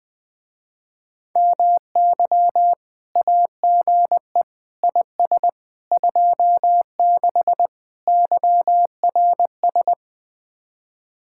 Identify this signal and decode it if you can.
Morse code, heard as a keyed tone, about 21 wpm, 700 Hz. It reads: MY AGE IS 26 YRS
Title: Speed-Racing Common QSO Phrases - 20wpm